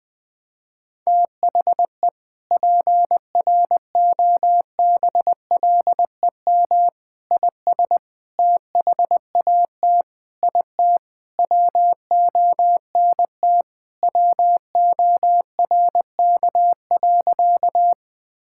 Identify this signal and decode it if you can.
Morse code, heard as a keyed tone, about 20 wpm, 700 Hz. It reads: THE PROBLEM IS THAT IT WONT WORK.
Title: Sentences with 3-Word Phrases - 20wpm